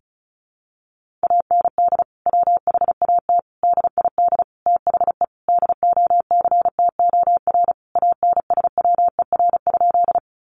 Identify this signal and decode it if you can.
Morse code, heard as a keyed tone, about 34 wpm, 700 Hz. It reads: AND WHAT DID THE DOCTOR ANSWER?
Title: Sentences from Top 1500 Words 35wpm